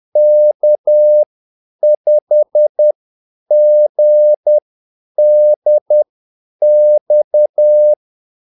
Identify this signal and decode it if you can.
Morse code, heard as a keyed tone, about 10 wpm, spaced as for 8 wpm, 600 Hz. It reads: K5GDX
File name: Callsigns-10-8wpm